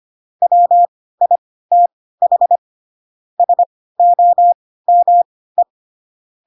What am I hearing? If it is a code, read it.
Morse code, heard as a keyed tone, about 25 wpm, spaced as for 16 wpm, 700 Hz. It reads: WITH SOME
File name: Pinocchio / Pinocchio ch7 10 25 ik1hge